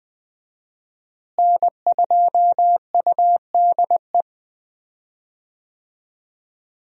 Morse code, heard as a keyed tone, about 20 wpm, 700 Hz. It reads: N2UDE
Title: USA Call Signs - 5 Characters - Repeated 3x - 20wpm